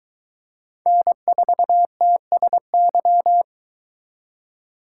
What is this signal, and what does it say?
Morse code, heard as a keyed tone, about 22 wpm, 700 Hz. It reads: N4TSY